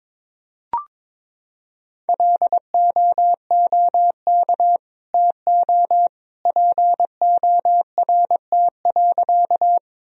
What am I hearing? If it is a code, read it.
Morse code, heard as a keyed tone, about 22 wpm, 700 Hz. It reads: LOOK TO PORT.
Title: Sentences from Top 1000 Words - Review Part 1 - 22wpm